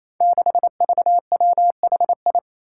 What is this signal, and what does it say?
Morse code, heard as a keyed tone, about 27 wpm, 700 Hz. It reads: 6VWHI
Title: Five Challenging Letter-Number Combinations 28wpm